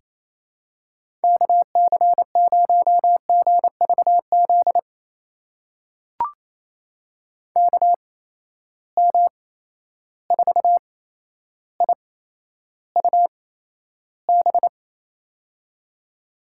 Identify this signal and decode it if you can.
Morse code, heard as a keyed tone, about 28 wpm, spaced as for 12 wpm, 700 Hz. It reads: KC0GVZ KM4IUB
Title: USA Call Signs - 6 Characters - 8x Character Spacing - 28wpm